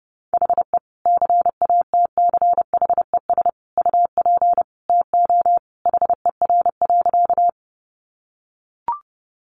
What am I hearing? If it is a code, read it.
Morse code, heard as a keyed tone, about 30 wpm, 700 Hz. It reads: HE CATCHES UP TO HER.
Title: Additional Easy Phrases 30wpm